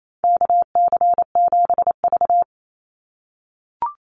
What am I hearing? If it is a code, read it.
Morse code, heard as a keyed tone, about 28 wpm, 700 Hz. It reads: KC7V
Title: Call Signs - 28wpm